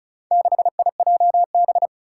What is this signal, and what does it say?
Morse code, heard as a keyed tone, about 34 wpm, 700 Hz. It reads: 6IJB
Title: Four Challenging Letter-Number Combinations 35wpm